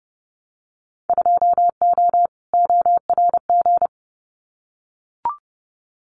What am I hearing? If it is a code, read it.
Morse code, heard as a keyed tone, about 30 wpm, 700 Hz. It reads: 2O ORG
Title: Winter Field Day Exchanges 30wpm